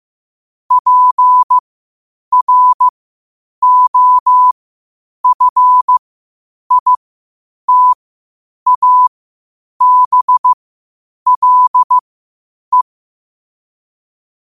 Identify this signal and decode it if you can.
Morse code, heard as a keyed tone, about 15 wpm, spaced as for 8 wpm, 1000 Hz. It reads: PROFITABLE